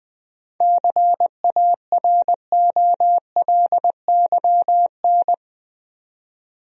Morse code, heard as a keyed tone, about 20 wpm, 700 Hz. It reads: CAROLYN
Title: USA Common Names - 20wpm